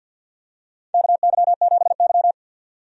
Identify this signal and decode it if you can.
Morse code, heard as a keyed tone, about 53 wpm, 700 Hz. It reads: KY7Y
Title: USA Call Signs - 4 Characters - 50wpm